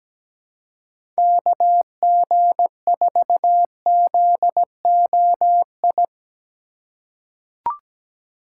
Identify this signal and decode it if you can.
Morse code, heard as a keyed tone, about 17 wpm, 700 Hz. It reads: KG4ZOI
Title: USA Call Signs - 6 Characters - 2x Character Spacing - 17wpm